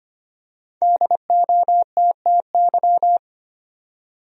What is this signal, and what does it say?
Morse code, heard as a keyed tone, about 25 wpm, 700 Hz. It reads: DOTTY
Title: USA Common Names - Max 5 Letters - 25wpm